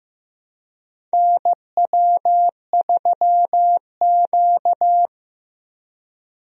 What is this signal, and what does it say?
Morse code, heard as a keyed tone, about 15 wpm, 700 Hz. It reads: NW3Q
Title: USA Call Signs - 4 Characters - 15wpm